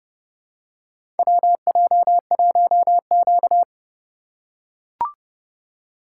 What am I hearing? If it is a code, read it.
Morse code, heard as a keyed tone, about 30 wpm, 700 Hz. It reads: WJ1Q
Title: USA Call Signs - 3x Character Spacing - 30wpm